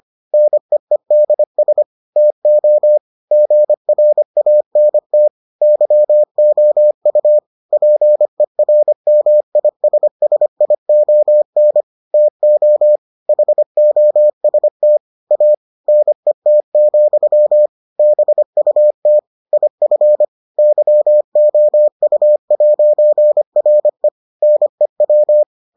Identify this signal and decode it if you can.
Morse code, heard as a keyed tone, about 25 wpm, 600 Hz. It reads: NEEDS TO GRANT YOU PERMISSION TO HOST A NET, BUT IF YOU'RE NEW